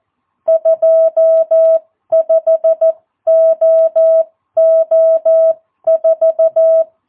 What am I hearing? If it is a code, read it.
Morse code, heard as a keyed tone, about 14 wpm, spaced as for 12 wpm, 660 Hz. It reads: 25OO4
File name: jh8cbh / 2022cw